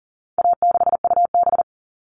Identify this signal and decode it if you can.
Morse code, heard as a keyed tone, about 40 wpm, 700 Hz. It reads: A6UB
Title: All Cut Numbers - 4-Digit 40wpm